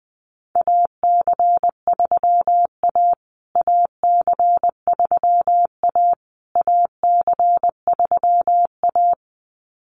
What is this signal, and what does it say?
Morse code, heard as a keyed tone, about 20 wpm, 700 Hz. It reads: AC3A AC3A AC3A
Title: USA Call Signs Repeated 3x 20wpm